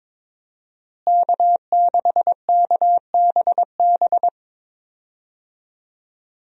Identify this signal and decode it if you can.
Morse code, heard as a keyed tone, about 22 wpm, 700 Hz. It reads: K6KBB
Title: USA Call Signs - 5 Characters - Repeated 3x - 22wpm